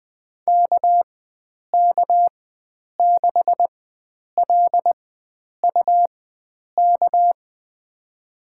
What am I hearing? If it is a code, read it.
Morse code, heard as a keyed tone, about 20 wpm, spaced as for 9 wpm, 700 Hz. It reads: KK6LUK
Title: USA Call Signs - 6 Characters - 4x Character Spacing - 20wpm